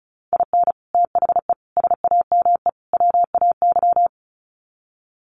Morse code, heard as a keyed tone, about 34 wpm, 700 Hz. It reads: IN THE SAME WAY
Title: Sets of 4 Words 35wpm